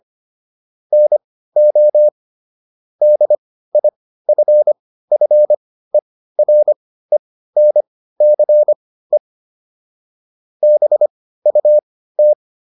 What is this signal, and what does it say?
Morse code, heard as a keyed tone, about 25 wpm, spaced as for 15 wpm, 600 Hz. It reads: NO DIFFERENCE BUT